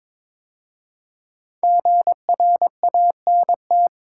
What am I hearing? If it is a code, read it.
Morse code, heard as a keyed tone, about 22 wpm, 700 Hz. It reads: GRANT
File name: 5 Letter Words - First 3 Letters Given - 22wpm